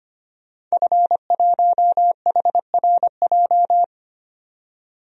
Morse code, heard as a keyed tone, about 25 wpm, 700 Hz. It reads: F1HRJ
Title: DXCC-100 Call Signs - 5 Characters - 25wpm